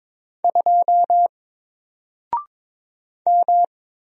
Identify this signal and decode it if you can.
Morse code, heard as a keyed tone, about 22 wpm, 700 Hz. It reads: 2 M
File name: Single-Letter-Number-22wpm